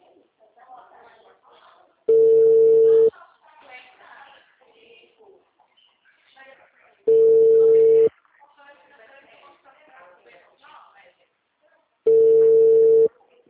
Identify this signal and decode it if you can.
Ringback tone (China, 450 Hz), with speech.
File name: out- -2016- - - .22768